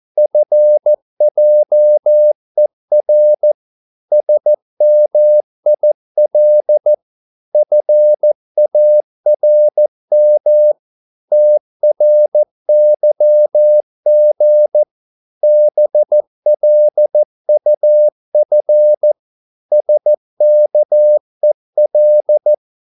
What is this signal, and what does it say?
Morse code, heard as a keyed tone, about 14 wpm, 600 Hz. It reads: FJER SMIL FARM TRYG BLUF SKEL
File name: Korte ord DK 14wpm